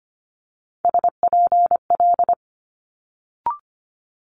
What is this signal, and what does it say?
Morse code, heard as a keyed tone, about 25 wpm, 700 Hz. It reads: SPL
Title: Multi-letter Phonemes Quick Fire 25wpm